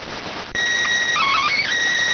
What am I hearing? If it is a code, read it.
SSTV (slow-scan television), PD 290.